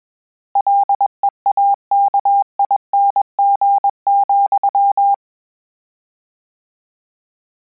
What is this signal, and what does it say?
Morse code, heard as a keyed tone, about 21 wpm, 800 Hz. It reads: LEAKING,
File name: Combinations / Combinations-mp3s / Combinations@20